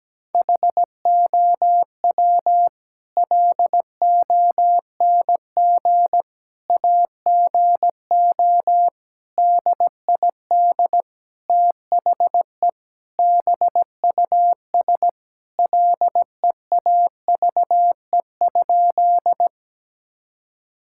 Morse code, heard as a keyed tone, about 17 wpm, 700 Hz. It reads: HOW LONG AGO DID THE BUS LEAVE?
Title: Sentences from Top 1000 Words - Review Part 6 - 17wpm